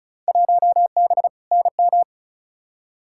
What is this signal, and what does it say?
Morse code, heard as a keyed tone, about 36 wpm, 700 Hz. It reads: 1B NM
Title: Field Day Exchanges Encore - 35wpm